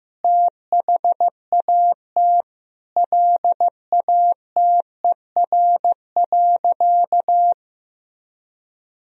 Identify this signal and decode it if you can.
Morse code, heard as a keyed tone, about 15 wpm, 700 Hz. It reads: THAT LATER.
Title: Sentences from Top 1000 Words - Review Part 4 - 15wpm